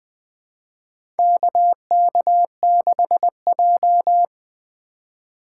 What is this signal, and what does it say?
Morse code, heard as a keyed tone, about 20 wpm, 700 Hz. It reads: KK6J
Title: USA Call Signs - 4 Characters - 2x Character Spacing - 20wpm